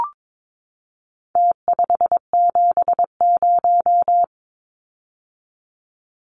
Morse code, heard as a keyed tone, about 22 wpm, 700 Hz. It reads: T570
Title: Numbers - Contest Style - 4-Digit 22wpm